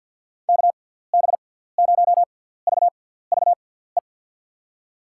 Morse code, heard as a keyed tone, about 47 wpm, spaced as for 17 wpm, 700 Hz. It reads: KB0VVE
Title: USA Call Signs - 6 Characters - 6x Character Spacing - 50wpm